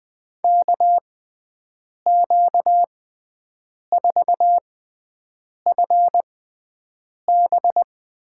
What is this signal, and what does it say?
Morse code, heard as a keyed tone, about 20 wpm, spaced as for 7 wpm, 700 Hz. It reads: KQ4FB
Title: USA Call Signs - 5 Characters - 6x Character Spacing - 20wpm